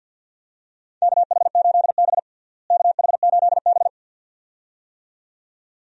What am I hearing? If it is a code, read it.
Morse code, heard as a keyed tone, about 50 wpm, 700 Hz. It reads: KH8B KH8B
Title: USA Call Signs - 4 Characters - Repeated 2x - 50wpm